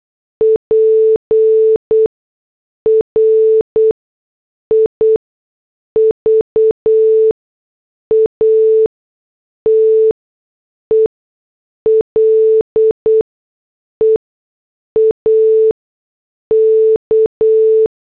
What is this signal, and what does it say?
Morse code, heard as a keyed tone, about 8 wpm, spaced as for 6 wpm, 440 Hz. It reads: PRIVATE LEAK
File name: morse / headlines